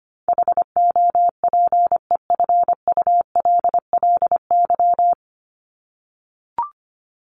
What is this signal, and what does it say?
Morse code, heard as a keyed tone, about 25 wpm, 700 Hz. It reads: HOPEFULLY
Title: Top 1251-1500 Words 25wpm